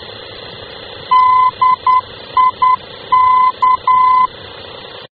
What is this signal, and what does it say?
Morse code, heard as a keyed tone, about 10 wpm, 1020 Hz. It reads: DIK
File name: luxembourg / mx / DIK-L-VOR